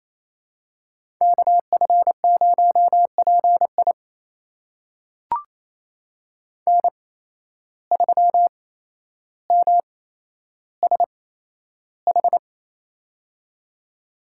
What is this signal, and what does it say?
Morse code, heard as a keyed tone, about 28 wpm, spaced as for 12 wpm, 700 Hz. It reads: KF0PI N3MSH
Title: USA Call Signs - 5 Characters - 8x Character Spacing - 28wpm